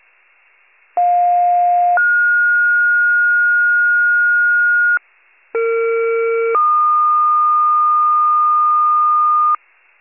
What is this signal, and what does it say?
Two-tone page, heard as a one-second tone, then a three-second tone, about 710 then 1430 Hz, about 460 then 1220 Hz, with noise. It is heard from 1.0 to 9.6 s.